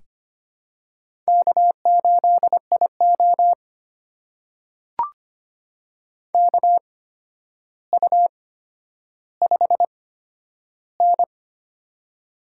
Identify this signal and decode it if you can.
Morse code, heard as a keyed tone, about 25 wpm, spaced as for 11 wpm, 700 Hz. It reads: K8IO KU5N
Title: USA Call Signs - 4 Characters - 8x Character Spacing - 25wpm